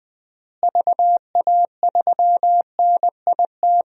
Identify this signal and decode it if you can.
Morse code, heard as a keyed tone, about 20 wpm, 700 Hz. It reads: VA3NIT
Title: DXCC-100 Call Signs - 6 Characters - Repeated 2x - 20wpm